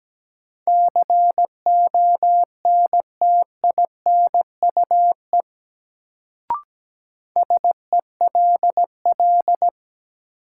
Morse code, heard as a keyed tone, about 17 wpm, 700 Hz. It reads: CONTINUE SELL
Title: Sentences from Top 1000 Words - Review Part 6 - New Word Drill - 17wpm